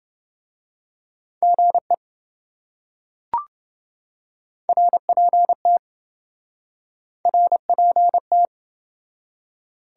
Morse code, heard as a keyed tone, about 26 wpm, 700 Hz. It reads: GE RPT RPT